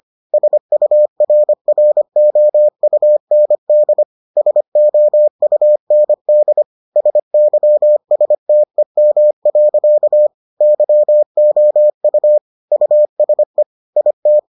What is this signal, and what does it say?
Morse code, heard as a keyed tone, about 25 wpm, 600 Hz. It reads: SURROUND SOUND SYSTEM. YOU USE IT